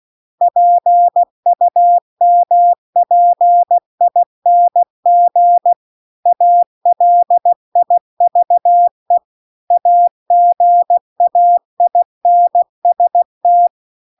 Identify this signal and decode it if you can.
Morse code, heard as a keyed tone, about 16 wpm, 700 Hz. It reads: PUMPING ALIVE AGAINST
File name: CW-The-Big-Twoheaded-River-2-16WPM